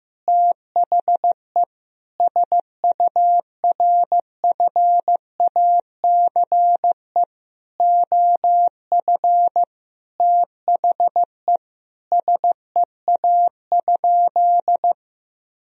Morse code, heard as a keyed tone, about 15 wpm, 700 Hz. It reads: THE SURFACE OF THE SEA?
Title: Sentences from Top 1000 Words - Review Part 3 - 15wpm